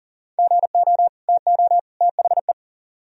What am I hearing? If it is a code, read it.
Morse code, heard as a keyed tone, about 52 wpm, 700 Hz. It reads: GO TO THE